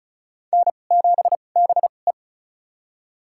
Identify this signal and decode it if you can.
Morse code, heard as a keyed tone, about 36 wpm, spaced as for 25 wpm, 700 Hz. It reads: N7BE
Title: USA Call Signs - 2x Character Spacing - 35wpm